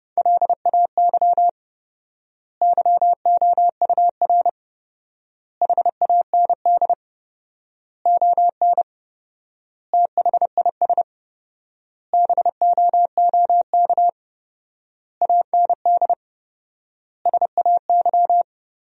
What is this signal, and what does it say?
Morse code, heard as a keyed tone, about 30 wpm, 700 Hz. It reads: LAY YOUR HAND ON THIS BOOK AND SAY
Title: 4x Word-Spacing - Sentences from Top 400 Words - 30wpm